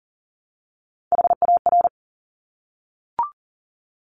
Morse code, heard as a keyed tone, about 42 wpm, 700 Hz. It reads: HAR
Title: Top Three Letter Combinations in Words 40wpm